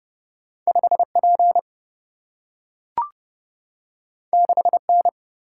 Morse code, heard as a keyed tone, about 30 wpm, 700 Hz. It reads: 5P 6N
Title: Single-Letter Single-Number 30wpm